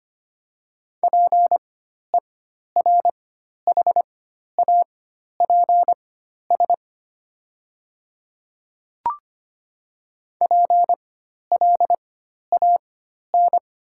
Morse code, heard as a keyed tone, about 25 wpm, spaced as for 12 wpm, 700 Hz. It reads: PERHAPS PLAN
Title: 4x Character Spacing - Top 500 Words - 25wpm